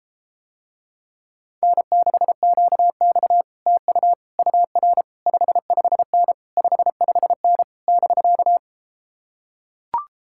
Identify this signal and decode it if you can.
Morse code, heard as a keyed tone, about 33 wpm, 700 Hz. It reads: N6QX TU UR 55N 55N <BK>